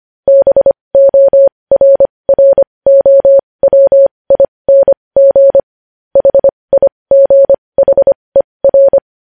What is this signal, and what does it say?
Morse code, heard as a keyed tone, about 25 wpm, spaced as for 20 wpm, 550 Hz. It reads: BORROWING HIGHER
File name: podcast / News headlines-1x-25x20